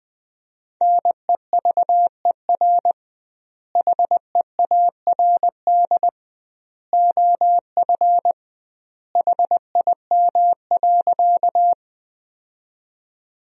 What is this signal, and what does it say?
Morse code, heard as a keyed tone, about 20 wpm, 700 Hz. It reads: NEVER HEARD OF HIM.
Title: 2x Word-Spacing - Sentences from Top 400 Words - 20wpm